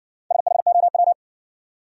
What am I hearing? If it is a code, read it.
Morse code, heard as a keyed tone, about 60 wpm, 700 Hz. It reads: SHOW